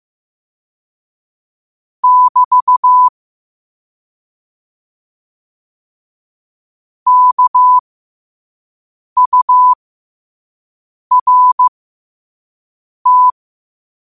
Morse code, heard as a keyed tone, about 15 wpm, spaced as for 5 wpm, 1000 Hz. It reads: <BT> KURT
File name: qotdfiles / 5wpm